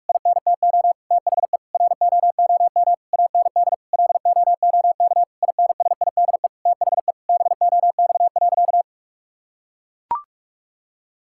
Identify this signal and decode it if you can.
Morse code, heard as a keyed tone, about 43 wpm, 700 Hz. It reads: INTO THE ROOM AND LOOK INSIDE THE BOX.
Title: Sentences from Top 1000 Words - Review Part 6 - 45wpm